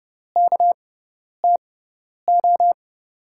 Morse code, heard as a keyed tone, about 31 wpm, spaced as for 10 wpm, 700 Hz. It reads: KTO
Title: USA Call Signs - 5 Characters - 6x Character Spacing - 30wpm